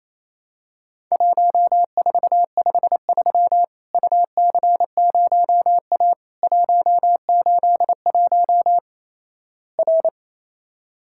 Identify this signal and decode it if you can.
Morse code, heard as a keyed tone, about 28 wpm, 700 Hz. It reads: 1453 UC0A 181 R
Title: Worked All Europe DX Contest - QTC Exchange - 28wpm